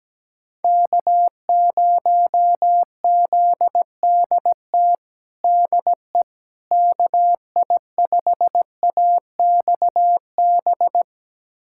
Morse code, heard as a keyed tone, about 17 wpm, 700 Hz. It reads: K0ZDT DE KI5AXB
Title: USA Callsign DE Callsign 17wpm